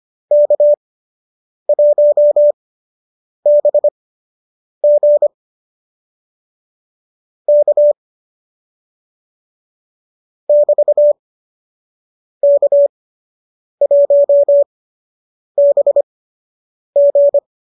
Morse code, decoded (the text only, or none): K1BG K <BT> K1BG